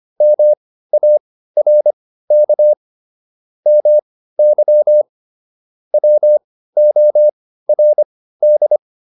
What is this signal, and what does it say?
Morse code, heard as a keyed tone, about 25 wpm, spaced as for 15 wpm, 600 Hz. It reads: MARK MY WORD